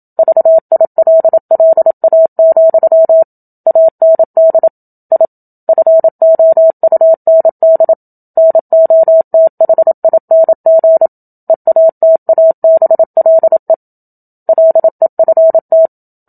Morse code, heard as a keyed tone, about 27 wpm, 650 Hz. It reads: VILLA, AND I FOUND NOTHING EATABLE LEFT
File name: Book Two / 01-Chapter 1 - 27 WPM